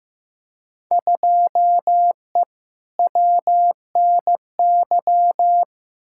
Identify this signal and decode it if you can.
Morse code, heard as a keyed tone, about 15 wpm, 700 Hz. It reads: 2E WNY
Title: Field Day Exchanges Encore - 15wpm